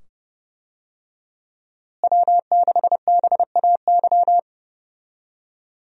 Morse code, heard as a keyed tone, about 30 wpm, 700 Hz. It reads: W6BAY
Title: USA Call Signs - 5 Characters - Repeated 2x - 30wpm